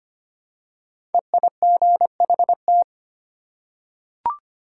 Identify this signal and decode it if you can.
Morse code, heard as a keyed tone, about 25 wpm, 700 Hz. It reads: EIGHT
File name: 5 Letter Words - First 3 Letters Given - 25wpm